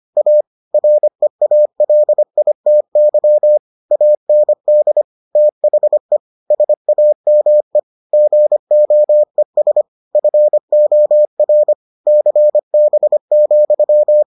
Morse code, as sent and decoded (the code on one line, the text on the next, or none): .- / .-. .- .-.. .. - -.-- / .- -. -.. / - .... / ... .- -- . / --. --- . ... / ..-. --- .-. / -.-. -... --..--
A REALITY AND THE SAME GOES FOR CB,